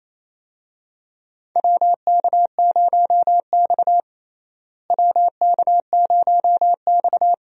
Morse code, heard as a keyed tone, about 28 wpm, 700 Hz. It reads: WK0X WK0X
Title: USA Call Signs - 4 Characters - Repeated 2x - 28wpm